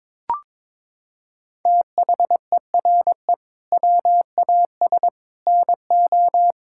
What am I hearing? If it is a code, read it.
Morse code, heard as a keyed tone, about 22 wpm, 700 Hz. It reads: THERE WAS NO
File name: Sentences with 7-Word Phrases - 22wpm